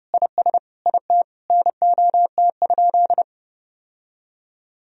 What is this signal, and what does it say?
Morse code, heard as a keyed tone, about 30 wpm, 700 Hz. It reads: IS IT NOT?
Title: Sentences from Top 1000 Words - Review Part 1 - 30wpm